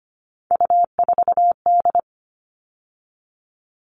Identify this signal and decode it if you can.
Morse code, heard as a keyed tone, about 25 wpm, 700 Hz. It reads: U4D
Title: All Cut Numbers - 3-Digit 25wpm